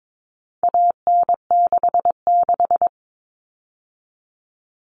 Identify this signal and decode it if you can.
Morse code, heard as a keyed tone, about 22 wpm, 700 Hz. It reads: AN66
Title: All Cut Numbers - 4-Digit 22wpm